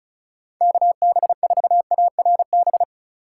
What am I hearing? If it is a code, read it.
Morse code, heard as a keyed tone, about 36 wpm, 700 Hz. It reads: KB4ARB
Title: USA Call Signs - 6 Characters - 4x Character Spacing - 35wpm